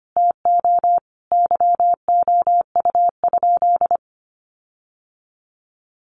Morse code, heard as a keyed tone, about 25 wpm, 700 Hz. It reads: TO YOU?
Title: Sentences from Top 1300 Words 25wpm